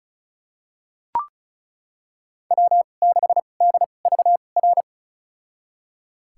Morse code, heard as a keyed tone, about 36 wpm, spaced as for 25 wpm, 700 Hz. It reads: W6DVR
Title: USA Call Signs - 5 Characters - 2x Character Spacing - 35wpm